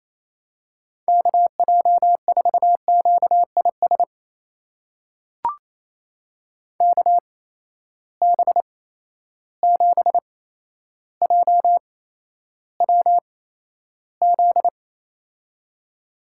Morse code, heard as a keyed tone, about 28 wpm, spaced as for 12 wpm, 700 Hz. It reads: KJ4QIS KB7JWZ
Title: USA Call Signs - 6 Characters - 8x Character Spacing - 28wpm